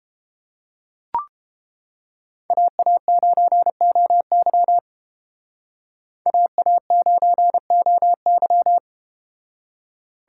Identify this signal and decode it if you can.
Morse code, heard as a keyed tone, about 30 wpm, 700 Hz. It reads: AA9OY AA9OY